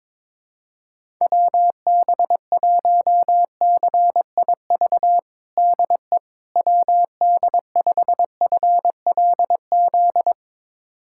Morse code, heard as a keyed tone, about 22 wpm, 700 Hz. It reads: WB1CIV DE WD5FLZ
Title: USA Callsign DE Callsign - 6 Characters - 22wpm